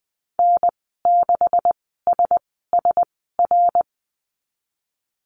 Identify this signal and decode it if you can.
Morse code, heard as a keyed tone, about 20 wpm, spaced as for 14 wpm, 700 Hz. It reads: N6SSR
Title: 140 Lesson 23g - Letter f - Call Signs 20wpm